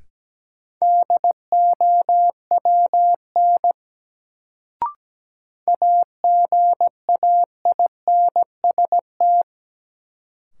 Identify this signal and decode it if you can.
Morse code, heard as a keyed tone, about 17 wpm, 700 Hz. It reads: DOWN AGAINST
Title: Sets of 2 Words - New Word Drill - 17wpm